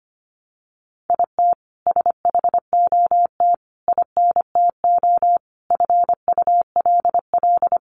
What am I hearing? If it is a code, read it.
Morse code, heard as a keyed tone, about 25 wpm, 700 Hz. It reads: IT SHOT INTO FULL
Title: Sentences from Top 1100 Words 25wpm